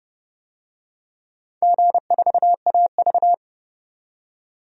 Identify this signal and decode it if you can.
Morse code, heard as a keyed tone, about 28 wpm, 700 Hz. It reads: G4AV
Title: DXCC-100 Call Signs - 4 Characters - 30wpm